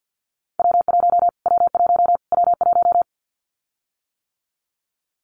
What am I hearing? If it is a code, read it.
Morse code, heard as a keyed tone, about 50 wpm, 700 Hz. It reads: W1 W1 W1
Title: Callsign Prefixes for Top 25 DXCC - Repeated 3x - 50wpm